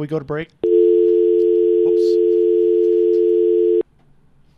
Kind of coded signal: telephone call progress tone